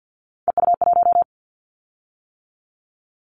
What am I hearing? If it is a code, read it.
Morse code, heard as a keyed tone, about 50 wpm, 700 Hz. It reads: EU1
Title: Callsign Prefixes for Top 100 DXCC - 50wpm